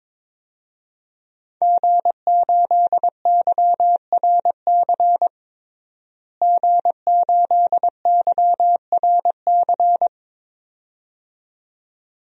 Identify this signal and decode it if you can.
Morse code, heard as a keyed tone, about 22 wpm, 700 Hz. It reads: G8YRC G8YRC
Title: DXCC-100 Call Signs - 5 Characters - Repeated 2x - 22wpm